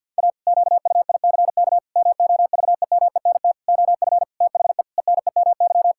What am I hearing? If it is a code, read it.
Morse code, heard as a keyed tone, about 51 wpm, 700 Hz. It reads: A QUICK MOVEMENT OF THE ENEMY
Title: Pangram Challenge v2 - 50wpm